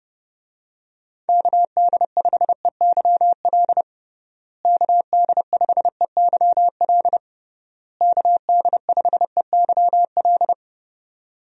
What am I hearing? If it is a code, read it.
Morse code, heard as a keyed tone, about 30 wpm, 700 Hz. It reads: KD5EYL KD5EYL KD5EYL
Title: USA Call Signs - 6 Characters - Repeated 3x - 30wpm